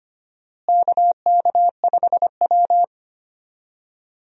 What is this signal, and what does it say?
Morse code, heard as a keyed tone, about 25 wpm, 700 Hz. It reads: KK5W
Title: USA Call Signs - 4 Characters - Repeated 2x - 25wpm